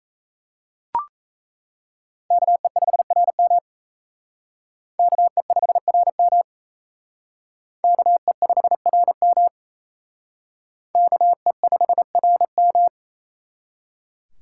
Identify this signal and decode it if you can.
Morse code, heard as a keyed tone, about 35 wpm, 700 Hz. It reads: KE5RM KE5RM KE5RM KE5RM
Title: Speed Racing - USA 2x2 Call Signs 28wpm